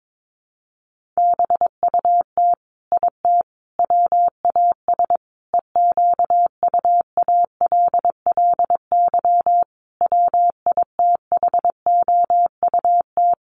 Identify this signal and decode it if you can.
Morse code, heard as a keyed tone, about 22 wpm, 700 Hz. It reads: BUT IT WAS EQUALLY WITHOUT
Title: Sentences from Top 1400 Words 22wpm